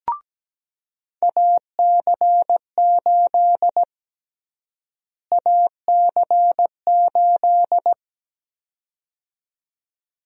Morse code, heard as a keyed tone, about 17 wpm, 700 Hz. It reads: AC8 AC8
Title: USA Call Sign Prefixes - Repeated 2x - 17wpm